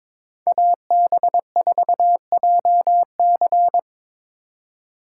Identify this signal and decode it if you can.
Morse code, heard as a keyed tone, about 22 wpm, 690 Hz. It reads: AB4JC
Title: USA Call Signs - 5 Characters - 2x Character Spacing - 22wpm